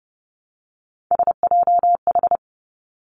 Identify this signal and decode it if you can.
Morse code, heard as a keyed tone, about 30 wpm, 700 Hz. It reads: SJH
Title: Three Challenging Letter-Number Combinations 30wpm